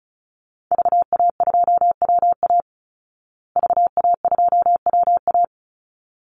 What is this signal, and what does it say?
Morse code, heard as a keyed tone, about 34 wpm, 700 Hz. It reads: VA2WA VA2WA
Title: Call Signs Repeated 3x - 35wpm